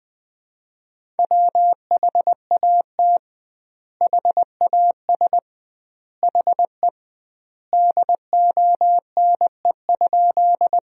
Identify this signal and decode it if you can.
Morse code, heard as a keyed tone, about 20 wpm, 700 Hz. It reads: WHAT HAS HE DONE?
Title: 2x Word-Spacing - Sentences from Top 500 Words - 20wpm